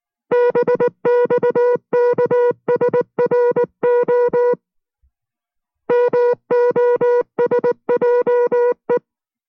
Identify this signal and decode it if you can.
Morse code, heard as a keyed tone, about 19 wpm, 490 Hz. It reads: BXKSRO MOSJE